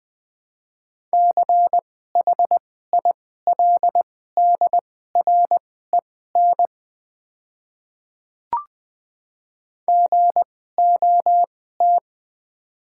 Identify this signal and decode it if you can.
Morse code, heard as a keyed tone, about 20 wpm, spaced as for 14 wpm, 700 Hz. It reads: CHILDREN GOT
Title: 2x Character Spacing - Top 300 Words - 20wpm